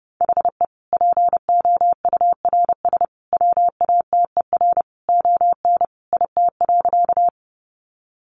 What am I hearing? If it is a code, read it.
Morse code, heard as a keyed tone, about 30 wpm, 700 Hz. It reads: HE POURS WATER ON IT.
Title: Additional Easy Phrases 30wpm